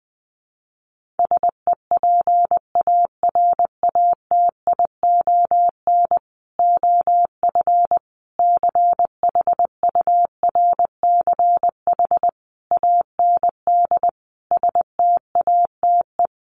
Morse code, heard as a keyed tone, about 20 wpm, 700 Hz. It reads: SEPARATION OF CHURCH AND STATE
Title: Sets of 5 Words 20wpm